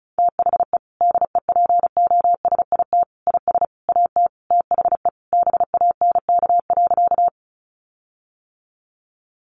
Morse code, heard as a keyed tone, about 34 wpm, 700 Hz. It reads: THE DEPOSIT IS AT THE BANK.
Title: Sentences from Top 1300 Words 35wpm